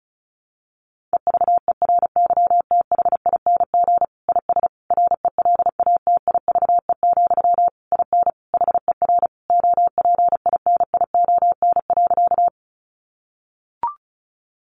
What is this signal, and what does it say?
Morse code, heard as a keyed tone, about 34 wpm, 700 Hz. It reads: EVERYTHING IS RELATIVE, IN HER OPINION.
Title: Sentences from Top 1400 Words 35wpm